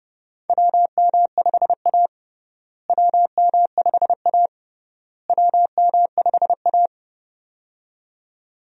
Morse code, heard as a keyed tone, about 32 wpm, 700 Hz. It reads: WM5A WM5A WM5A
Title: USA Call Signs - 4 Characters - Repeated 3x - 30wpm